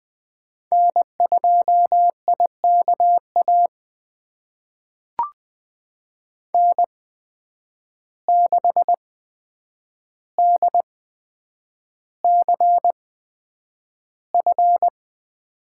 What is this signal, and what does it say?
Morse code, heard as a keyed tone, about 20 wpm, spaced as for 9 wpm, 700 Hz. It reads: N2IKA N6DCF
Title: USA Call Signs - 5 Characters - 8x Character Spacing - 20wpm